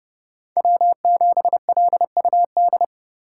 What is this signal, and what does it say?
Morse code, heard as a keyed tone, about 29 wpm, 700 Hz. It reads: W7LUD